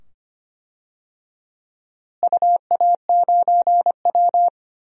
Morse code, heard as a keyed tone, about 25 wpm, 700 Hz. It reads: UA9W
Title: DXCC-100 Call Signs - 4 Characters - Repeated 2x - 25wpm